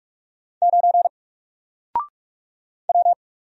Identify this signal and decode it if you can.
Morse code, heard as a keyed tone, about 44 wpm, 700 Hz. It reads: 9 W
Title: Single Character (Letter:Number:Punctuation:Prosign) 45wpm